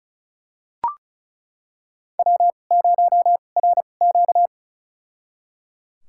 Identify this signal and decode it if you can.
Morse code, heard as a keyed tone, about 34 wpm, spaced as for 25 wpm, 700 Hz. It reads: W0RQ
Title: USA Call Signs - 4 Characters - 2x Character Spacing - 35wpm